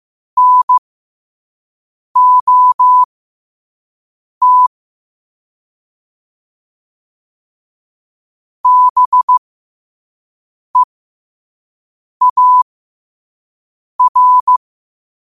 Morse code, heard as a keyed tone, about 15 wpm, spaced as for 5 wpm, 1000 Hz. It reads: NOT BEAR
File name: qotdfiles / 5wpm